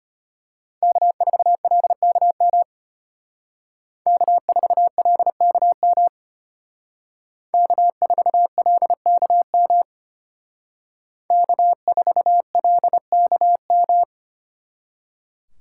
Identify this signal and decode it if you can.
Morse code, heard as a keyed tone, about 32 wpm, 700 Hz. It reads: K4LKM K4LKM K4LKM K4LKM